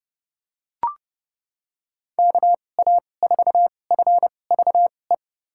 Morse code, heard as a keyed tone, about 30 wpm, spaced as for 22 wpm, 700 Hz. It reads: KA4FVE
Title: USA Call Signs - 6 Characters - 2x Character Spacing - 30wpm